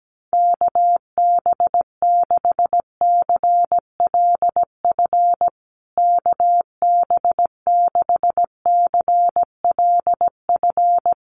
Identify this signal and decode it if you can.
Morse code, heard as a keyed tone, about 17 wpm, 700 Hz. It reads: KB6CLF KB6CLF
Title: USA Call Signs Repeated 3x 17wpm